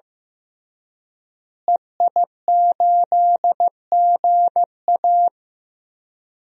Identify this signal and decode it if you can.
Morse code, heard as a keyed tone, about 16 wpm, 700 Hz. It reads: EI8GA
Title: DXCC-100 Call Signs - 5 Characters - Repeated 3x - 15wpm